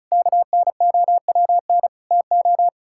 Morse code, heard as a keyed tone, about 36 wpm, 700 Hz. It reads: KNOWN TO
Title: Sentences from Top 1000 Words - Review Part 5 - 35wpm